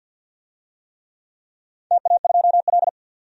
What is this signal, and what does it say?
Morse code, heard as a keyed tone, about 50 wpm, 700 Hz. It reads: TA2L
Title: DXCC-100 Call Signs - 4 Characters - 50wpm